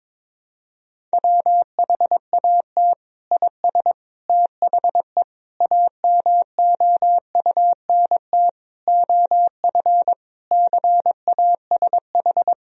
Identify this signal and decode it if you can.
Morse code, heard as a keyed tone, about 22 wpm, 700 Hz. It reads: WHAT IS THE AMOUNT OF CASH
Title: Sentences from Top 1000 Words - Review Part 4 - 22wpm